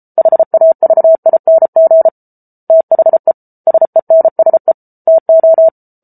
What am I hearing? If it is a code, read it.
Morse code, heard as a keyed tone, about 34 wpm, 650 Hz. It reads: HAVING THE SENSE TO